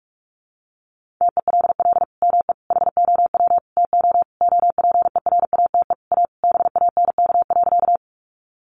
Morse code, heard as a keyed tone, about 44 wpm, 700 Hz. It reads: TELL ME HOW TO OPERATE A BANK.
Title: Sentences from Top 1500 Words 45wpm